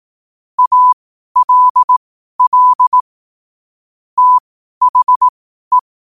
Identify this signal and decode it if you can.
Morse code, heard as a keyed tone, about 18 wpm, spaced as for 12 wpm, 1000 Hz. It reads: ALL THE
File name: qotdfiles / 12wpm